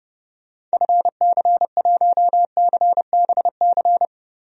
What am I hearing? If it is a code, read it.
Morse code, heard as a keyed tone, about 32 wpm, 700 Hz. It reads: FC1CBC
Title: DXCC-100 Call Signs - 6 Characters - 30wpm